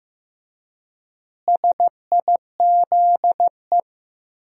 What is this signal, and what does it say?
Morse code, heard as a keyed tone, about 15 wpm, 700 Hz. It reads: SIZE
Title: 4x Word-Spacing - Sentences from Top 400 Words - 15wpm